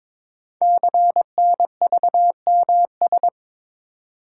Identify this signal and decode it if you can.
Morse code, heard as a keyed tone, about 22 wpm, 700 Hz. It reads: CNVMS
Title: Code Groups - 5 Letters - 22wpm